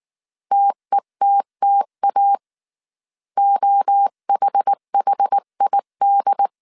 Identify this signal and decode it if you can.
Morse code, heard as a keyed tone, about 19 wpm, 800 Hz. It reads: TETTA OHHID
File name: CW / DET / 1 3 1500C S B 18wpm 1916